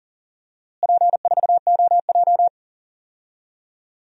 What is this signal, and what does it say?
Morse code, heard as a keyed tone, about 40 wpm, 700 Hz. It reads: P4OJ
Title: DXCC-100 Call Signs - 4 Characters - Repeated 3x - 40wpm